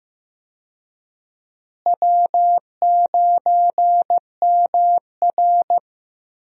Morse code, heard as a keyed tone, about 15 wpm, 700 Hz. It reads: W9MR
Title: USA Call Signs - 4 Characters - Repeated 3x - 15wpm